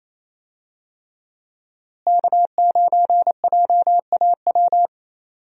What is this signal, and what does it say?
Morse code, heard as a keyed tone, about 28 wpm, 700 Hz. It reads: K9JAW
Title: USA Call Signs - 5 Characters - Repeated 3x - 28wpm